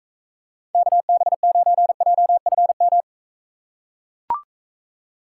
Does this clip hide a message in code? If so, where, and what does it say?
Yes: Morse code from 0.7 to 3.0 s, heard as a keyed tone, about 44 wpm, 700 Hz. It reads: KB9JFM